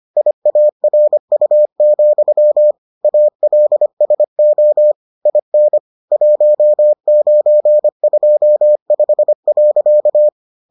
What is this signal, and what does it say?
Morse code, heard as a keyed tone, about 25 wpm, 600 Hz. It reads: IARU, ALSO IN 1925.